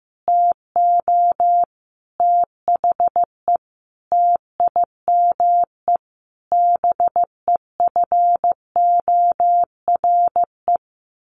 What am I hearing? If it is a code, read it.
Morse code, heard as a keyed tone, about 15 wpm, 700 Hz. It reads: TO THE TIME BEFORE
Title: A Sound Of Thunder 15wpm